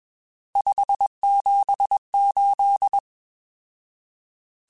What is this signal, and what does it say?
Morse code, heard as a keyed tone, about 21 wpm, 800 Hz. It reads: 578
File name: RSTs / RST-mp3s / Various RST @20